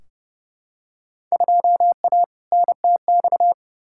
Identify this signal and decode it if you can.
Morse code, heard as a keyed tone, about 30 wpm, 700 Hz. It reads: 2A NTX
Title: Field Day Exchanges - 30wpm